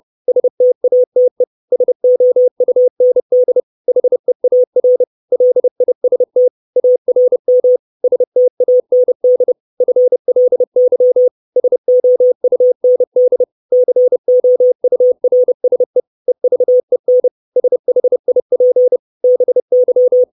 Morse code, heard as a keyed tone, about 30 wpm, 500 Hz. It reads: STATE SOUND HEAR LIST ARM STAND FLY SOUND COURSE EVEN SHIP BY
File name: 632 Words 30WPM